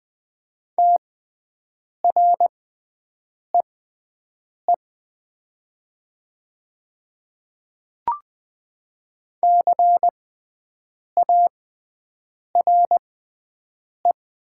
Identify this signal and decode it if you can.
Morse code, heard as a keyed tone, about 20 wpm, spaced as for 7 wpm, 700 Hz. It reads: TREE CARE
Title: 6x Character Spacing - Top 500 Words - 20wpm